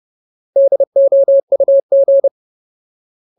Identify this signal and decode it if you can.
Morse code, heard as a keyed tone, about 30 wpm, 550 Hz. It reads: DOUG
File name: Ultimate Word List / 30WPM Ultimate